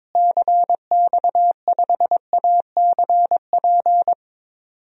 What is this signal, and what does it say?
Morse code, heard as a keyed tone, about 22 wpm, 700 Hz. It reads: CX5ACP
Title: DXCC-100 Call Signs - 6 Characters - Repeated 2x - 22wpm